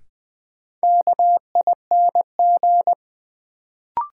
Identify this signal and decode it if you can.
Morse code, heard as a keyed tone, about 20 wpm, 700 Hz. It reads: KING
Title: Sentences from Top 400 Words - New Word Drill - 20wpm